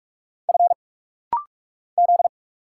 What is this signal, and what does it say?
Morse code, heard as a keyed tone, about 46 wpm, 700 Hz. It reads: F Z
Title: Single Letters - Rapid-Fire 45wpm